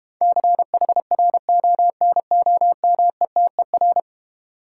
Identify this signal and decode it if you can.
Morse code, heard as a keyed tone, about 32 wpm, 690 Hz. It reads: CHRONOMETER